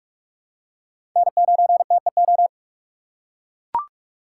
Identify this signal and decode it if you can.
Morse code, heard as a keyed tone, about 42 wpm, 700 Hz. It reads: N9TEO
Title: USA Call Signs - 2x Character Spacing - 45wpm